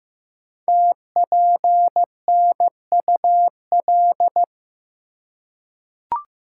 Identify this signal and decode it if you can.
Morse code, heard as a keyed tone, about 15 wpm, 700 Hz. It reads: TPNUL
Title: Code Groups - 5 Letters - 15wpm